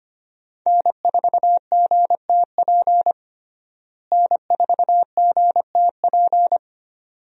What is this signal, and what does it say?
Morse code, heard as a keyed tone, about 25 wpm, 700 Hz. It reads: N4GTP N4GTP
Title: USA Call Signs - 5 Characters - Repeated 2x - 25wpm